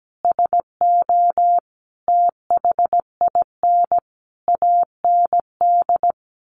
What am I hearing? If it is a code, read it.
Morse code, heard as a keyed tone, about 17 wpm, 700 Hz. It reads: SO THIN AND
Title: Sentences from Top 1200 Words 17wpm